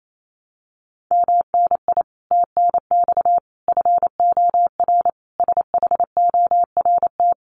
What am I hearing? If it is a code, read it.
Morse code, heard as a keyed tone, about 28 wpm, 700 Hz. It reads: MNI TNX FOR SHORT
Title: Common QSO Phrases 28wpm